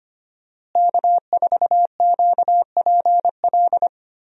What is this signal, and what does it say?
Morse code, heard as a keyed tone, about 25 wpm, 700 Hz. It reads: K4QPL
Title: USA Call Signs - 5 Characters - Repeated 2x - 25wpm